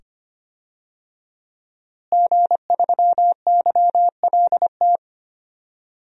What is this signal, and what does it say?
Morse code, heard as a keyed tone, about 25 wpm, 700 Hz. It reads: G3YLT
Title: DXCC-100 Call Signs - 5 Characters - Repeated 2x - 25wpm